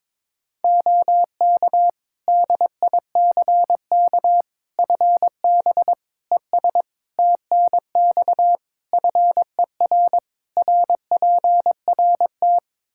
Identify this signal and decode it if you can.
Morse code, heard as a keyed tone, about 22 wpm, 700 Hz. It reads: OK DICK FB ES TNX FER RPRT
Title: Practice QSO - Exchange 2 - Info - 22wpm